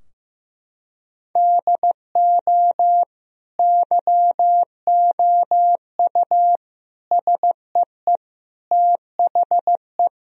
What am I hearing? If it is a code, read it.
Morse code, heard as a keyed tone, about 15 wpm, 700 Hz. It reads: DO YOU SEE THE
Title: Sentences from Top 1000 Words - Review Part 4 - 15wpm